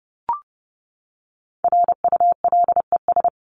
Morse code, heard as a keyed tone, about 30 wpm, 700 Hz. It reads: RULES
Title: ES:S Suffix - Top 100 Words with Base Words 30wpm